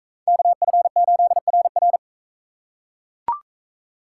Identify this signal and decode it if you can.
Morse code, heard as a keyed tone, about 39 wpm, 700 Hz. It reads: KF8RR